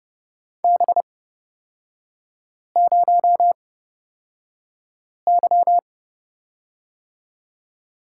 Morse code, heard as a keyed tone, about 31 wpm, spaced as for 5 wpm, 700 Hz. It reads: B0Y
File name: Lesson 38e - Letter q - Three Characters 30wpm